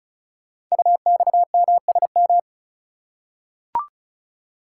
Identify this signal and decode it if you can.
Morse code, heard as a keyed tone, about 36 wpm, 700 Hz. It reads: UXMSM
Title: Code Groups - 5 Letters - 35wpm